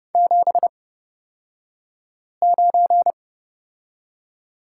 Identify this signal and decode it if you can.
Morse code, heard as a keyed tone, about 30 wpm, spaced as for 5 wpm, 700 Hz. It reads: 79
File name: Lesson 38d - Letter q - Two Characters 30wpm